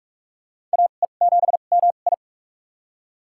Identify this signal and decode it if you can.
Morse code, heard as a keyed tone, about 46 wpm, spaced as for 32 wpm, 700 Hz. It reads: AE7MI
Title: USA Call Signs - 5 Characters - 2x Character Spacing - 45wpm